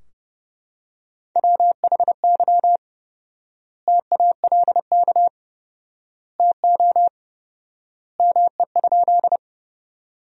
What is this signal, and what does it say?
Morse code, heard as a keyed tone, about 30 wpm, 700 Hz. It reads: WHY TALK TO ME?
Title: 4x Word-Spacing - Sentences from Top 400 Words - 30wpm